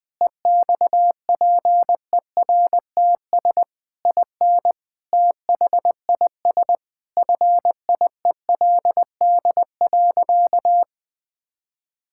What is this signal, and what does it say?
Morse code, heard as a keyed tone, about 20 wpm, 700 Hz. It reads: EXPERTS IN THIS FIELD.
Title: Sentences with 4-Word Phrases - Spoken Beginning - 20wpm